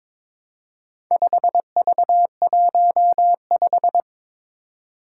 Morse code, heard as a keyed tone, about 22 wpm, 700 Hz. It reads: 5V15
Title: Four Challenging Letter-Number Combinations 22wpm